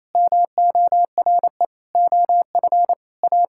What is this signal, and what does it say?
Morse code, heard as a keyed tone, about 28 wpm, 700 Hz. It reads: MORE OF A
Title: Sentences from Top 1000 Words - Review Part 5 - 28wpm